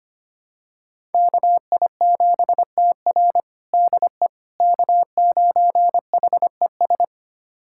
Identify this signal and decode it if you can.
Morse code, heard as a keyed tone, about 25 wpm, 700 Hz. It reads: KI7TR DE K9HES
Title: USA Callsign DE Callsign - 5 Characters - 25wpm